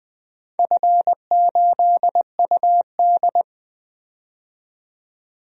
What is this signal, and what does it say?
Morse code, heard as a keyed tone, about 20 wpm, 700 Hz. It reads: F8UD
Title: DXCC-100 Call Signs - 4 Characters - 20wpm